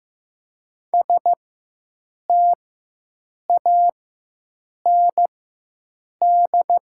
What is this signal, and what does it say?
Morse code, heard as a keyed tone, about 15 wpm, spaced as for 7 wpm, 700 Hz. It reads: STAND
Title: 4x Character Spacing - Top 500 Words - 15wpm